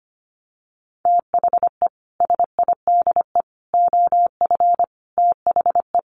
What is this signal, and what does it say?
Morse code, heard as a keyed tone, about 25 wpm, 700 Hz. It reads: THE SIDE OF THE
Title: Sets of 5 Words 25wpm